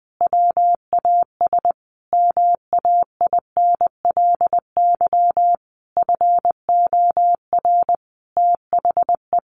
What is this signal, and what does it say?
Morse code, heard as a keyed tone, about 19 wpm, 700 Hz. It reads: WAS MAINLY FOR THE
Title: Sentences from Top 1400 Words 20wpm